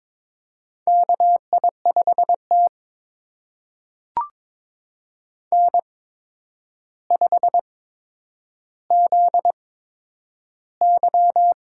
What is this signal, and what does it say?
Morse code, heard as a keyed tone, about 22 wpm, spaced as for 9 wpm, 700 Hz. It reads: KI5T N5ZY
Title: USA Call Signs - 4 Characters - 8x Character Spacing - 22wpm